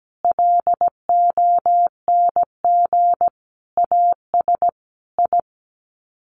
Morse code, heard as a keyed tone, about 17 wpm, 700 Hz. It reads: LONG AS I
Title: Sets of 5 Words Encore2 17wpm